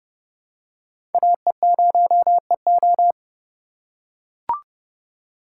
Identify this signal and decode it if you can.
Morse code, heard as a keyed tone, about 30 wpm, 700 Hz. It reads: AE0EO
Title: USA Call Signs - 5 Characters - 2x Character Spacing - 30wpm